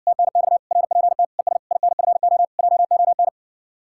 Morse code, heard as a keyed tone, about 60 wpm, 700 Hz. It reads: TNX RPT ES INFO JON